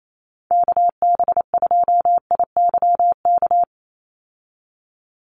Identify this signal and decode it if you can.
Morse code, heard as a keyed tone, about 28 wpm, 700 Hz. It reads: KB2IYK
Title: USA Call Signs 28wpm